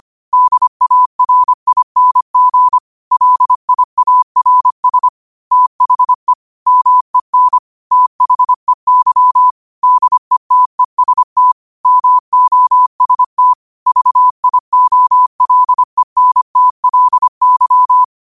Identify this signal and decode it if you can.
Morse code, heard as a keyed tone, about 25 wpm, 1000 Hz. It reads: DARING LIARS THE MEN THEY DETEST MOST VIOLENTLY
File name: qotdfiles / 25wpm